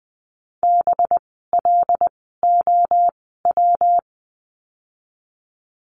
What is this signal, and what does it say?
Morse code, heard as a keyed tone, about 20 wpm, spaced as for 15 wpm, 700 Hz. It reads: BLOW
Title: 185 Lesson 30f - Letter b - Words 20wpm